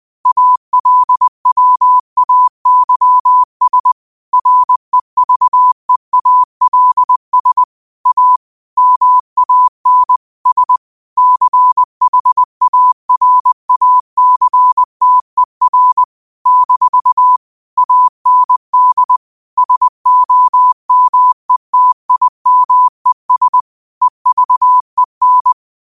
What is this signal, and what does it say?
Morse code, heard as a keyed tone, about 20 wpm, 1000 Hz. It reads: ALWAYS REVEALS A MAN S CHARACTER - AND SOMETIMES EVEN